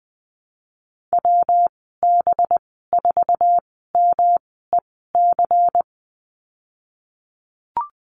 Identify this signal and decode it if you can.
Morse code, heard as a keyed tone, about 20 wpm, spaced as for 15 wpm, 700 Hz. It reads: WB4MEC
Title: 193 Lesson 31g - Letter v - Call Signs 20wpm